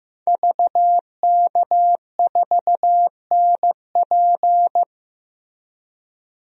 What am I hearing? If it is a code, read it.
Morse code, heard as a keyed tone, about 16 wpm, 700 Hz. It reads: VK4NP
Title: DXCC-100 Call Signs - 5 Characters - Repeated 3x - 15wpm